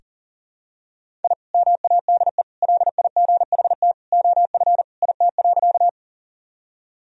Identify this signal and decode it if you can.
Morse code, heard as a keyed tone, about 37 wpm, 700 Hz. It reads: I MADE LIGHT OF IT.
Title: Sentences from Top 1000 Words - Review Part 1 - 40wpm